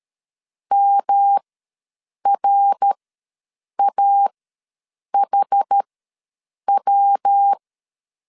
Morse code, heard as a keyed tone, about 13 wpm, spaced as for 7 wpm, 800 Hz. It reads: MRAHW
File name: LEZIONI / 5 LBVC / 5 LBVC ws 12wpm 3159